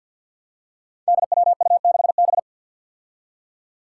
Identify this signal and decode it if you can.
Morse code, heard as a keyed tone, about 53 wpm, 700 Hz. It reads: DWU6B
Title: Five Challenging Letter-Number Combinations 50wpm